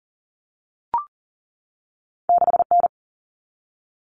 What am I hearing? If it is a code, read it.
Morse code, heard as a keyed tone, about 40 wpm, 700 Hz. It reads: 6N
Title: Common Cut Numbers - 2-Digit 40wpm